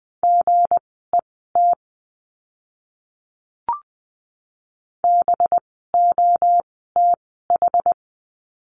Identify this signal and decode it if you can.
Morse code, heard as a keyed tone, about 20 wpm, spaced as for 14 wpm, 700 Hz. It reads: GET BOTH
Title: 185 Lesson 30f - Letter b - Words 20wpm